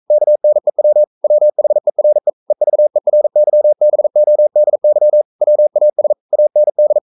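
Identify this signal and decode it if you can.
Morse code, heard as a keyed tone, about 42 wpm, 600 Hz. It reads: KNEW WHERE EVERYBODY WAS AND
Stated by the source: Morse code sending of an English novel at 42 wpm.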